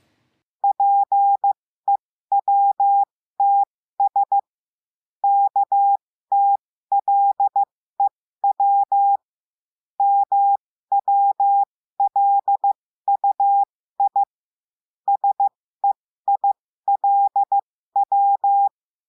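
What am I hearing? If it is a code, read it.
Morse code, heard as a keyed tone, about 15 wpm, spaced as for 13 wpm, 800 Hz. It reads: PEWTS KTLEW MWLUI SEILW
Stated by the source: Morse code lesson adding the letter I, with five-letter code groups.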